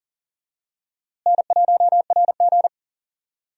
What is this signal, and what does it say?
Morse code, heard as a keyed tone, about 40 wpm, 700 Hz. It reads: N1RG
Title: USA Call Signs - 4 Characters - Repeated 2x - 40wpm